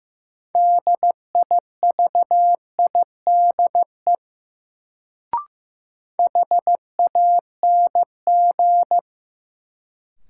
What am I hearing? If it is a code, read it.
Morse code, heard as a keyed tone, about 15 wpm, 700 Hz. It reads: DIVIDE HANG
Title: Sentences from Top 1600 Words - New Word Drill - 15wpm